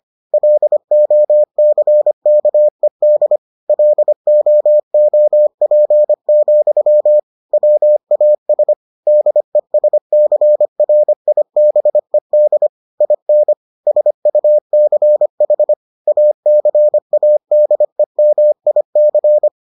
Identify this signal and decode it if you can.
Morse code, heard as a keyed tone, about 25 wpm, 600 Hz. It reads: LOCKED LOOP, WAS DESCRIBED IN SUCH ACADEMIC